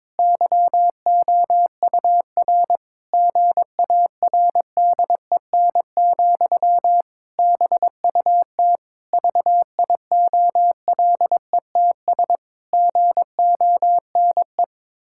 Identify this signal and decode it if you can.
Morse code, heard as a keyed tone, about 21 wpm, 700 Hz. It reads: YOUR GARDEN, BUT VIOLETS GONE
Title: Pangram Challenge v2 - 22wpm